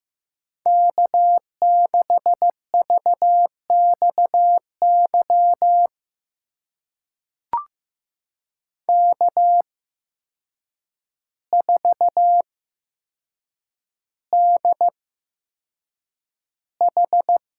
Morse code, heard as a keyed tone, about 16 wpm, spaced as for 7 wpm, 700 Hz. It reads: K6VXY K4DH
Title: USA Call Signs - 8x Character Spacing - 15wpm